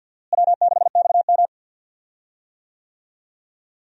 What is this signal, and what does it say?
Morse code, heard as a keyed tone, about 52 wpm, 700 Hz. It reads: W6XM
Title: USA Call Signs - 4 Characters - Repeated 2x - 50wpm